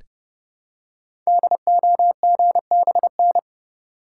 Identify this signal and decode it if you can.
Morse code, heard as a keyed tone, about 30 wpm, 700 Hz. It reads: DOGBN
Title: Code Groups - 5 Letters - 30wpm